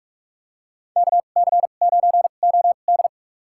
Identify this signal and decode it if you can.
Morse code, heard as a keyed tone, about 43 wpm, spaced as for 32 wpm, 700 Hz. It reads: KC9OD
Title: USA Call Signs - 5 Characters - 2x Character Spacing - 45wpm